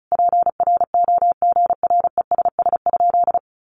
Morse code, heard as a keyed tone, about 34 wpm, 700 Hz. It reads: PROGRESS?
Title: Sentences from Top 1400 Words 35wpm